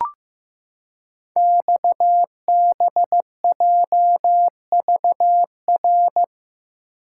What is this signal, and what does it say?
Morse code, heard as a keyed tone, about 15 wpm, 700 Hz. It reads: XBJVR
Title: Code Groups - 5 Letters - 15wpm